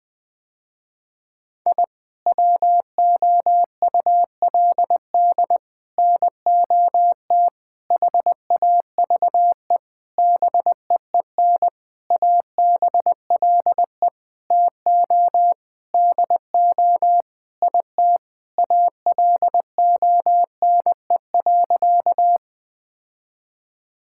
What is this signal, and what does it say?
Morse code, heard as a keyed tone, about 20 wpm, 700 Hz. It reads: I WOULD NOT HAVE BEEN ABLE TO DO IT ALONE.
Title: Sentences with 6-Word Phrases - Spoken Beginning - 20wpm